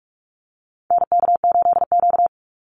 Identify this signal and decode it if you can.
Morse code, heard as a keyed tone, about 44 wpm, 700 Hz. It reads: NK8Q
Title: NA SOTA Activator Callsigns 45wpm